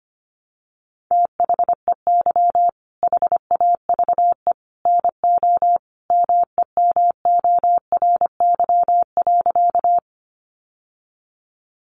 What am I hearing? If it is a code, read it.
Morse code, heard as a keyed tone, about 25 wpm, 700 Hz. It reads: THEY HAVE NO MEMORY.
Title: Sentences from Top 1300 Words 25wpm